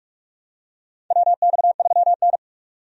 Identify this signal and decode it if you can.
Morse code, heard as a keyed tone, about 46 wpm, 700 Hz. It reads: WX3N
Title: USA Call Signs - 4 Characters - 45wpm